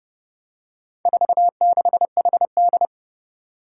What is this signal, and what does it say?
Morse code, heard as a keyed tone, about 30 wpm, 700 Hz. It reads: 46HD
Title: Four Challenging Letter-Number Combinations 30wpm